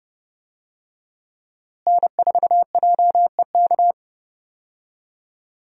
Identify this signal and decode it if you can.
Morse code, heard as a keyed tone, about 31 wpm, 700 Hz. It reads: N4JEK
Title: USA Call Signs - 5 Characters - Repeated 2x - 30wpm